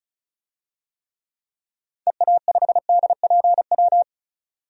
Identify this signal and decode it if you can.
Morse code, heard as a keyed tone, about 36 wpm, 700 Hz. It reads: EA5DPW
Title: DXCC-100 Call Signs - 6 Characters - 35wpm